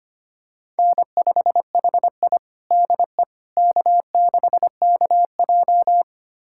Morse code, heard as a keyed tone, about 25 wpm, 700 Hz. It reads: N5HI DE K6KJ
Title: USA Callsign DE Callsign - 4 Characters - 25wpm